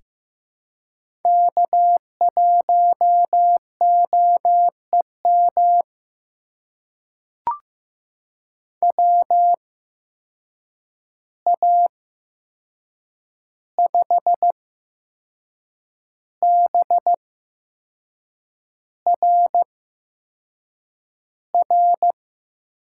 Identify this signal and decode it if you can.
Morse code, heard as a keyed tone, about 15 wpm, spaced as for 6 wpm, 700 Hz. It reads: K1OEM WA5BRR
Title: USA Call Signs - 8x Character Spacing - 15wpm